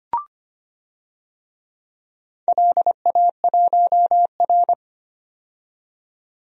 Morse code, heard as a keyed tone, about 25 wpm, 700 Hz. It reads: LA1R
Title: DXCC-100 Call Signs - 4 Characters - 25wpm